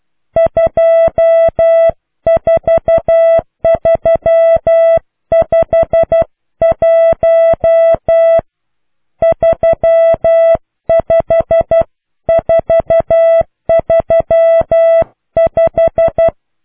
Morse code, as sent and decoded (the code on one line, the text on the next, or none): ..--- ....- ...-- ..... .---- / ...-- ..... ....- ...-- .....
24351 35435